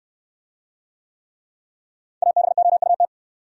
Morse code, heard as a keyed tone, about 68 wpm, 700 Hz. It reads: ABOUT